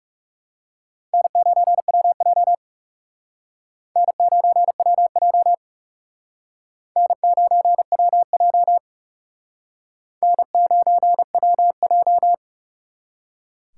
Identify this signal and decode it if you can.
Morse code, heard as a keyed tone, about 37 wpm, 700 Hz. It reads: N9WJ N9WJ N9WJ N9WJ
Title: Speed Racing - USA 1x2 Call Signs 30wpm